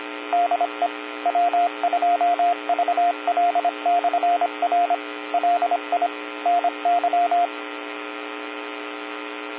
Morse code, heard as a keyed tone, about 25 wpm, 700 Hz, with radio noise. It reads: DE W2VL/R LI NY